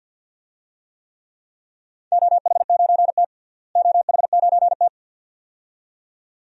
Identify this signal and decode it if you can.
Morse code, heard as a keyed tone, about 51 wpm, 700 Hz. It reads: OH9T OH9T
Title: DXCC-100 Call Signs - 4 Characters - Repeated 2x - 50wpm